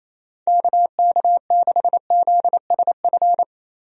Morse code, heard as a keyed tone, about 28 wpm, 700 Hz. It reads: KK6ZSF
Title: USA Call Signs - 6 Characters - Repeated 2x - 28wpm